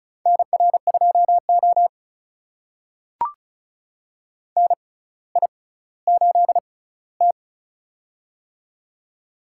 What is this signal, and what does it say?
Morse code, heard as a keyed tone, about 35 wpm, spaced as for 18 wpm, 700 Hz. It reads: NR2O NI8T
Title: USA Call Signs - 6x Character Spacing - 35wpm